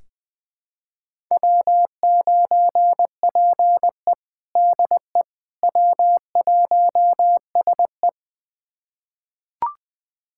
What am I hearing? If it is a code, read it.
Morse code, heard as a keyed tone, about 21 wpm, 700 Hz. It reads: W9PE DE W1SE
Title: USA Callsign DE Callsign - 4 Characters - 20wpm